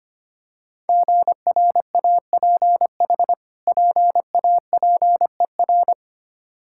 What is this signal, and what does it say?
Morse code, heard as a keyed tone, about 25 wpm, 700 Hz. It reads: GRAPH PAPER
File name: practice / 25wpm / Difficult Words with Hints - 25wpm